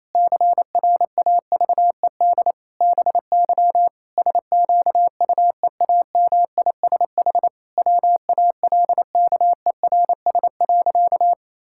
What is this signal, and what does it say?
Morse code, heard as a keyed tone, about 28 wpm, 700 Hz. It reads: CRAVED BY SQUEAMISH WALKERS.
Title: Pangram Challenge v2 - 28wpm